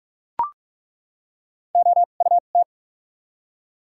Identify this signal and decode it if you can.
Morse code, heard as a keyed tone, about 43 wpm, spaced as for 32 wpm, 700 Hz. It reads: OUT
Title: 2x Character Spacing - Top 100 Words - 45wpm